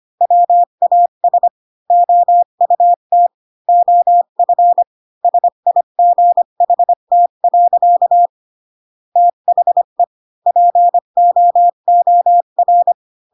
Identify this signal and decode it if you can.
Morse code, heard as a keyed tone, about 25 wpm, 700 Hz. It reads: WAS OUT OF SIGHT. THE POOR
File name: Pinocchio / Pinocchio ch5 20 25 ik1hge